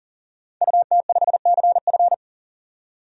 Morse code, heard as a keyed tone, about 40 wpm, 700 Hz. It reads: UT5CF
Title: DXCC-100 Call Signs - 5 Characters - 40wpm